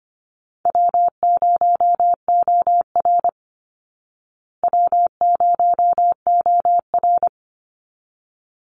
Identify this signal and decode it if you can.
Morse code, heard as a keyed tone, about 25 wpm, 700 Hz. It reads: W0OR W0OR
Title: Call Signs Repeated 3x - 25wpm